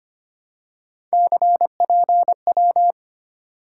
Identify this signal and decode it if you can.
Morse code, heard as a keyed tone, about 25 wpm, 700 Hz. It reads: CPW
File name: Three Letter-Number Combinations - 25wpm